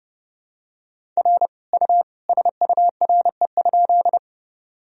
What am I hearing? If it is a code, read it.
Morse code, heard as a keyed tone, about 30 wpm, 700 Hz. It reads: R U SURE?
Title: Sentences from Top 300 Words and Abbreviations 30wpm